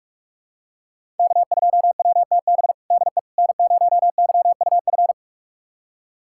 Morse code, heard as a keyed tone, about 43 wpm, 700 Hz. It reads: K2WTB DE N0YUF
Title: USA Callsign DE Callsign - 5 Characters - 45wpm